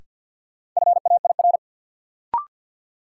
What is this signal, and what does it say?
Morse code, heard as a keyed tone, about 51 wpm, 700 Hz. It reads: FAIR
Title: Sentences from Top 1000 Words - Review Part 5 - New Word Drill - 50wpm